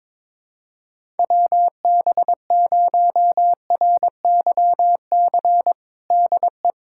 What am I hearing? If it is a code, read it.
Morse code, heard as a keyed tone, about 22 wpm, 700 Hz. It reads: WB0RYC DE
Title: USA Callsign DE Callsign 22wpm